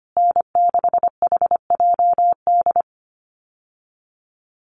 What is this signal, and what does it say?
Morse code, heard as a keyed tone, about 25 wpm, 700 Hz. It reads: N6HJD
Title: USA Call Signs Repeated 3x 25wpm